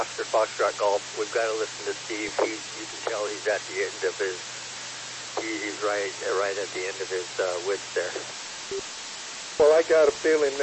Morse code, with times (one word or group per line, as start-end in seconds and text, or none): none